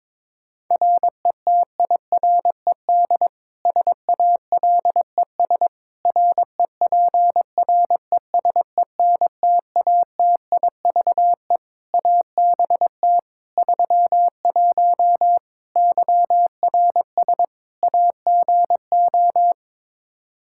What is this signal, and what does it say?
Morse code, heard as a keyed tone, about 22 wpm, 700 Hz. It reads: RETIRED SALES REPRESENTATIVE ABT 31 YRS AGO
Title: Practice QSO - Exchange 3 - More Info - 22wpm